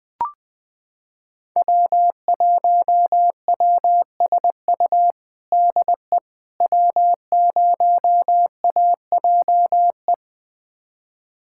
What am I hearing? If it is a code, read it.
Morse code, heard as a keyed tone, about 20 wpm, 700 Hz. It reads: W1WSU DE W0AJE
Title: USA Callsign DE Callsign - 5 Characters - 20wpm